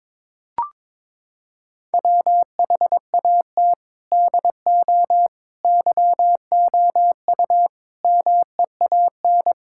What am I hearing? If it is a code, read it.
Morse code, heard as a keyed tone, about 22 wpm, 700 Hz. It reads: WHAT DO YOU MEAN
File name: practice / 22wpm / Sentences with 4-Word Phrases - 22wpm